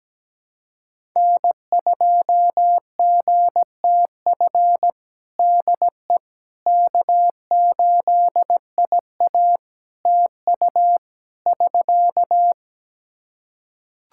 Morse code, heard as a keyed tone, about 17 wpm, 700 Hz. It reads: N2GTF DE K8IA TU <SK>
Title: Practice QSO - Exchange 4 - Ending - 17wpm